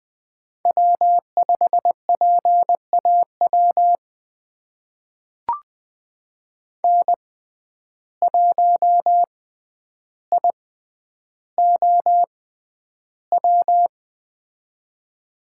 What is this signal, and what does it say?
Morse code, heard as a keyed tone, about 20 wpm, spaced as for 10 wpm, 700 Hz. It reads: W5PAW N1IOW
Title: USA Call Signs - 5 Characters - 6x Character Spacing - 20wpm